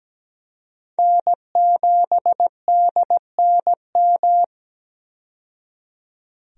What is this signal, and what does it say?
Morse code, heard as a keyed tone, about 17 wpm, 700 Hz. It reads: N7DNM
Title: USA Call Signs - 5 Characters - 17wpm